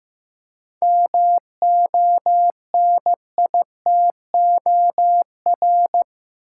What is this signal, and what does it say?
Morse code, heard as a keyed tone, about 15 wpm, 700 Hz. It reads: MONITOR
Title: Seven Letter Words - 15wpm